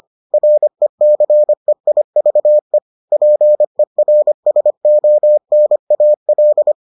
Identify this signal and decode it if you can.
Morse code, heard as a keyed tone, about 25 wpm, 600 Hz. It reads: RECEIVE PERSONAL